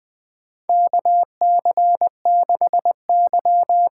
Morse code, heard as a keyed tone, about 20 wpm, 700 Hz. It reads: KC6Y
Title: USA Call Signs - 6 Characters - 6x Character Spacing - 20wpm